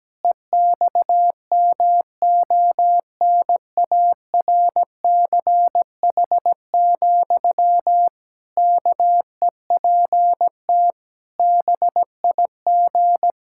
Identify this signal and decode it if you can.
Morse code, heard as a keyed tone, about 17 wpm, 700 Hz. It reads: EXMONARCH, KEPT BIG
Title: Pangram Challenge v2 - 17wpm